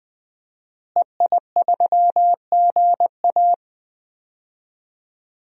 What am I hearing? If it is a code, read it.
Morse code, heard as a keyed tone, about 21 wpm, 700 Hz. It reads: EI3GA
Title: DXCC-100 Call Signs - 5 Characters - Repeated 2x - 20wpm